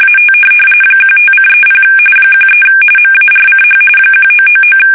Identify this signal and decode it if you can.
RTTY, about 45.45 baud. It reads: AT 15:30 13 MARCH 1999
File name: fsk850 / USS-Massachusetts